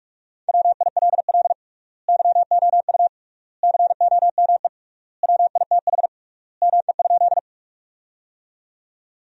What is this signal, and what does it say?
Morse code, heard as a keyed tone, about 46 wpm, 700 Hz. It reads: WILL YOU COME WITH ME?
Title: 3x Word-Spacing - Sentences from Top 200 Words - 45wpm